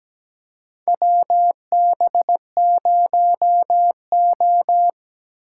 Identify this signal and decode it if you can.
Morse code, heard as a keyed tone, about 17 wpm, 700 Hz. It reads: WB0O
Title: USA Call Signs - 8x Character Spacing - 17wpm